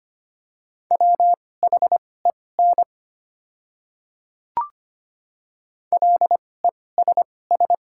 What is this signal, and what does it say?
Morse code, heard as a keyed tone, about 25 wpm, spaced as for 18 wpm, 690 Hz. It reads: WHEN LESS
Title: 2x Character Spacing - Top 500 Words - 25wpm